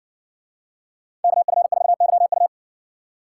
Morse code, heard as a keyed tone, about 60 wpm, 700 Hz. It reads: KF4YU